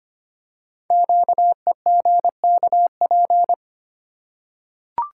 Morse code, heard as a keyed tone, about 25 wpm, 700 Hz. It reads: QEGKP
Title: Code Groups - 5 Letters - 25wpm